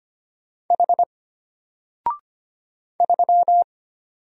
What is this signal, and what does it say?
Morse code, heard as a keyed tone, about 25 wpm, 700 Hz. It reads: H 3
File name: Single-Letter-Number-25wpm